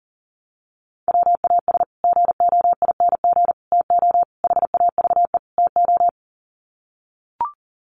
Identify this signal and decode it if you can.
Morse code, heard as a keyed tone, about 40 wpm, 700 Hz. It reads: WAS GOING TO HAVE TO
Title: Sets of 5 Words Encore 40wpm